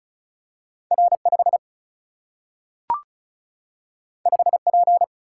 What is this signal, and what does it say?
Morse code, heard as a keyed tone, about 36 wpm, 700 Hz. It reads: R5 5P
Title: Single-Letter Single-Number 35wpm